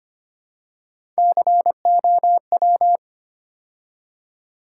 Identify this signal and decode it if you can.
Morse code, heard as a keyed tone, about 30 wpm, 700 Hz. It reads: COW
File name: Easy Words - Speed-Racing with Hints - 25wpm